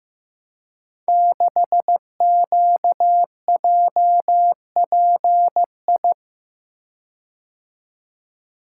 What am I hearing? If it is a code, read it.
Morse code, heard as a keyed tone, about 16 wpm, 700 Hz. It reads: 6QJPI
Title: Code Groups - 5 Letter:Numbers - 15wpm